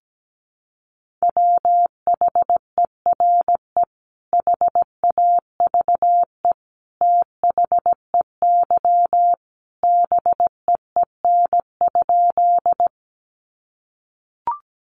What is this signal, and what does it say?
Morse code, heard as a keyed tone, about 17 wpm, 700 Hz. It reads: WHERE HAVE THEY BEEN?
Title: Sentences from Top 200 Words 17wpm